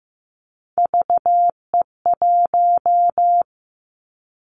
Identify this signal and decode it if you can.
Morse code, heard as a keyed tone, about 15 wpm, 700 Hz. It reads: VE1
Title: Callsign Prefixes for Top 100 DXCC - 15wpm